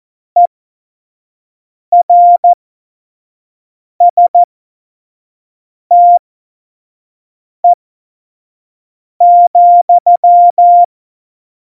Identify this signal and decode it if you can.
Morse code, heard as a keyed tone, about 14 wpm, spaced as for 5 wpm, 700 Hz. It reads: ERSTE,
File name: Morsekurs / Witze / Witz 13. November 2018, 25 F70